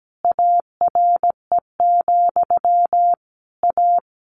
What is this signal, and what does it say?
Morse code, heard as a keyed tone, about 17 wpm, 700 Hz. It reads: ARE, A
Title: A Sound Of Thunder 17wpm